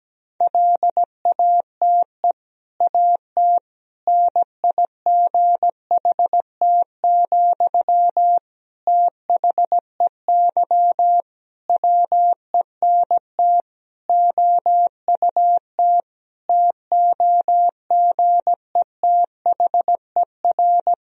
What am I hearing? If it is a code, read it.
Morse code, heard as a keyed tone, about 17 wpm, 700 Hz. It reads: LATE AT NIGHT, THEY WENT OUT TOGETHER
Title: Sentences from Top 1000 Words - Review Part 3 - 17wpm